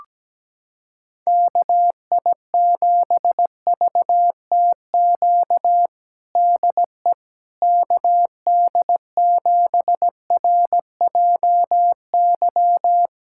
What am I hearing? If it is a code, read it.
Morse code, heard as a keyed tone, about 17 wpm, 700 Hz. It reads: KI7VTQ DE KD7RJY
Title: USA Callsign DE Callsign - 6 Characters - 17wpm